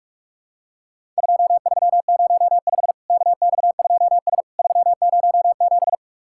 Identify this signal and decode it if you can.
Morse code, heard as a keyed tone, about 44 wpm, 700 Hz. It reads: 2305 KX2S 307
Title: Worked All Europe DX Contest - QTC Exchange - 45wpm